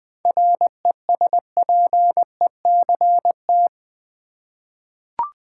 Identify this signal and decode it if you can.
Morse code, heard as a keyed tone, about 20 wpm, 700 Hz. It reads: RESPECT
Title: Seven Letter Words - 20wpm